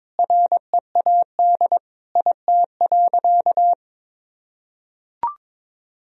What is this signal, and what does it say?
Morse code, heard as a keyed tone, about 21 wpm, 700 Hz. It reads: READ IT.
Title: Sentences from Top 500 Words and Abbreviations 22wpm